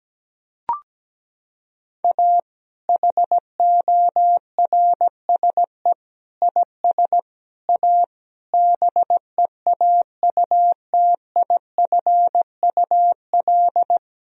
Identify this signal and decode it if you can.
Morse code, heard as a keyed tone, about 17 wpm, 700 Hz. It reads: A HORSE IS A BEAUTIFUL
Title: Sentences from Top 1000 Words - Review Part 1 - 17wpm